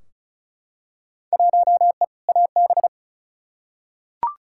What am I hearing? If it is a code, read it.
Morse code, heard as a keyed tone, about 36 wpm, 700 Hz. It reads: 1E AB
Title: Field Day Exchanges - 35wpm